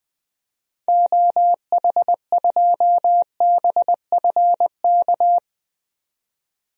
Morse code, heard as a keyed tone, about 20 wpm, 700 Hz. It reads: OH2BFK
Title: DXCC-100 Call Signs - 6 Characters - 20wpm